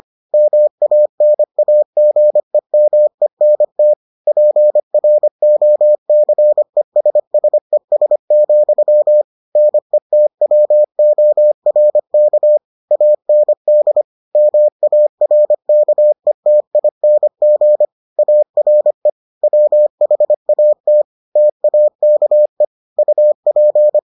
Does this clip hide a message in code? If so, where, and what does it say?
Yes: Morse code from 0.3 to 24.0 s, heard as a keyed tone, about 25 wpm, 600 Hz. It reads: MANAGEMENT PROCESSES, NETWORK AND MARKETING ARE WHAT TAKE UP